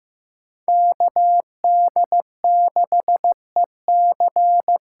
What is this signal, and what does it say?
Morse code, heard as a keyed tone, about 15 wpm, 700 Hz. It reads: KD6EC
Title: USA Call Signs - 6 Characters - 8x Character Spacing - 15wpm